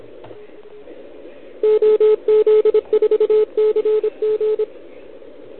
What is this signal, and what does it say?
Morse code, heard as a keyed tone, about 25 wpm, 410 Hz. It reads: OZ4CG
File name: ARCHIV / OKDX2020 / OL4Wa